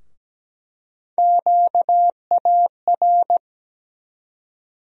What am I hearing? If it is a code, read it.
Morse code, heard as a keyed tone, about 17 wpm, 700 Hz. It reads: QAR
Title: USA Call Sign Postfixes - 3 Letters 17wpm